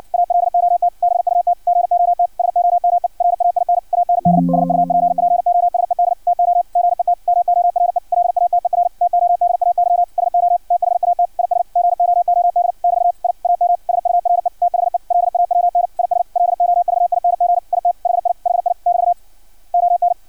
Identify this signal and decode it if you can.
Morse code, heard as a keyed tone, about 56 wpm, 700 Hz. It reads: ALOT BUT GOT SOME RAIN AND SUPPOSED TO GET MORE LATER TODAY SO THAT IS GOOD <BT> I AM SURE THE LAWN IS LOVING IT HI HI <BT> ON
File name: archived / CRSnet